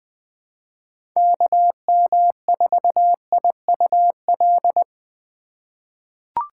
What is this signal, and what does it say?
Morse code, heard as a keyed tone, about 20 wpm, 700 Hz. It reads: KM4IUL
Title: USA Call Signs - 6 Characters - 3x Character Spacing - 20wpm